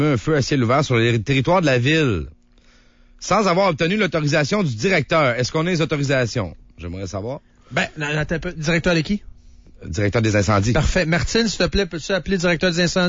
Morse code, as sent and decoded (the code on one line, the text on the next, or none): none